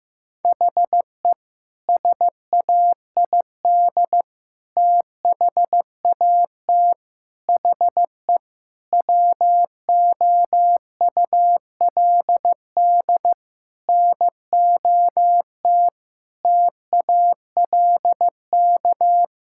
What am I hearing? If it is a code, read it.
Morse code, heard as a keyed tone, about 15 wpm, 700 Hz. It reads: HE SAID THAT HE WOULD NOT TALK
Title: Sentences from Top 1000 Words - Review Part 6 - 15wpm